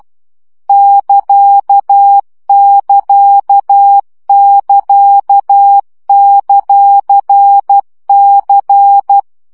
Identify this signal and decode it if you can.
Morse code, heard as a keyed tone, about 12 wpm, 800 Hz. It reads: <KA><KA><KA>;C
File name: CW gyakorlas c ct pontosvesszo 10 baud